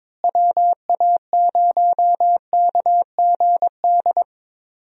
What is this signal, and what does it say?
Morse code, heard as a keyed tone, about 22 wpm, 700 Hz. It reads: WA0KGD
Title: USA Call Signs - 6 Characters - Repeated 2x - 22wpm